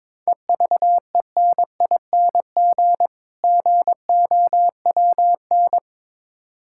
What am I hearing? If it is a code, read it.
Morse code, heard as a keyed tone, about 22 wpm, 690 Hz. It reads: EVENING GOWN
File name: Difficult Words with Hints - 22wpm